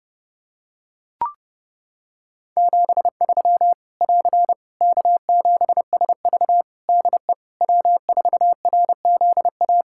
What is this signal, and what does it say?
Morse code, heard as a keyed tone, about 30 wpm, 700 Hz. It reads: 73 <AR> K7SV DE W4RZA
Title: Practice QSO - Exchange 4 - Ending - 30wpm